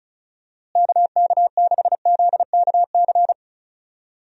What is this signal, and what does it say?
Morse code, heard as a keyed tone, about 34 wpm, 700 Hz. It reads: KK6ZKC
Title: USA Call Signs - 6 Characters - Repeated 3x - 35wpm